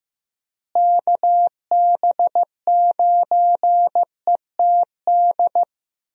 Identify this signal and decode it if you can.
Morse code, heard as a keyed tone, about 16 wpm, 700 Hz. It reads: KB9ETD
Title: USA Call Signs - 6 Characters - Repeated 3x - 15wpm